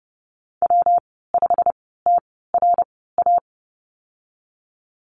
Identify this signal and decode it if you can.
Morse code, heard as a keyed tone, about 30 wpm, spaced as for 17 wpm, 700 Hz. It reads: W5TRA
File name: Lesson 19g - Letter w - Call Signs 30wpm